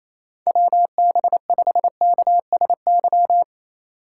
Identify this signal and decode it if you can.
Morse code, heard as a keyed tone, about 28 wpm, 690 Hz. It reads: WB5KSY
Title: USA Call Signs - 6 Characters - 2x Character Spacing - 28wpm